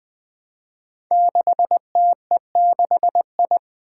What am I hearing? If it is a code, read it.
Morse code, heard as a keyed tone, about 20 wpm, 700 Hz. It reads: 6TE6I
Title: Five Challenging Letter-Number Combinations 20wpm